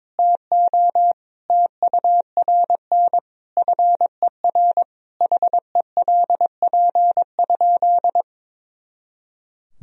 Morse code, heard as a keyed tone, about 22 wpm, 700 Hz. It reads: TO TURN FER HELP?
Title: Sentences from Top 200 Words and Abbreviations 22wpm